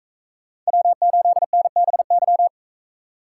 Morse code, heard as a keyed tone, about 41 wpm, 700 Hz. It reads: W8NBY